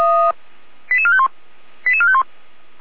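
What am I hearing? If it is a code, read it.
Five-tone selective call (ZVEI), heard as quick runs of five tones, tones of about 70 ms. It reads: 88451 88451
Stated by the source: ZVEI five-tone selective call 88199, received by SDR, from a server of Feuerwehr Orlamünde (fire brigade).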